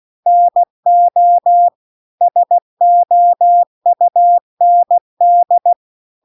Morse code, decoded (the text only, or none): NO SOUND